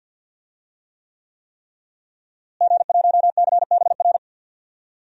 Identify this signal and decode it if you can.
Morse code, heard as a keyed tone, about 46 wpm, 700 Hz. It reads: G1CBR
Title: DXCC-100 Call Signs - 5 Characters - Repeated 3x - 50wpm